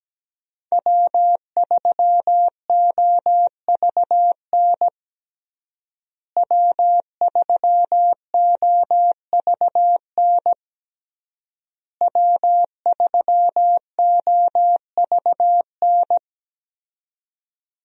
Morse code, heard as a keyed tone, about 17 wpm, 700 Hz. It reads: W3OVN W3OVN W3OVN
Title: USA Call Signs - 5 Characters - Repeated 3x - 17wpm